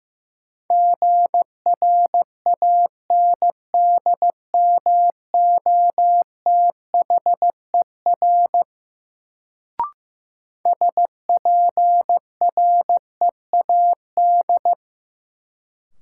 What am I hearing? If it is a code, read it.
Morse code, heard as a keyed tone, about 15 wpm, 700 Hz. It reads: GRANDMOTHER SPREAD
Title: Sentences from Top 1200 Words - New Word Drill - 15wpm